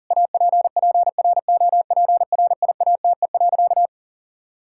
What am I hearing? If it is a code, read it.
Morse code, heard as a keyed tone, about 37 wpm, 700 Hz. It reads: APPROPRIATE.
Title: Sentences from Top 1600 Words 40wpm